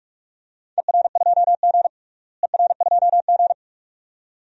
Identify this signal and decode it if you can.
Morse code, heard as a keyed tone, about 46 wpm, 700 Hz. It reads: ER2G ER2G
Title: DXCC-100 Call Signs - 4 Characters - Repeated 2x - 45wpm